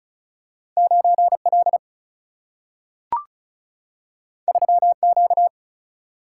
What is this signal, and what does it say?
Morse code, heard as a keyed tone, about 34 wpm, 700 Hz. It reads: 9L 3Q
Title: Single-Letter Single-Number 35wpm